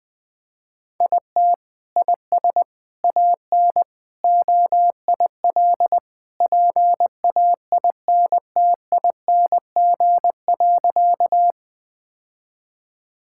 Morse code, heard as a keyed tone, about 21 wpm, 700 Hz. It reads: IT IS AN OIL PAINTING.
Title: Sentences from Top 1000 Words - Review Part 6 - 20wpm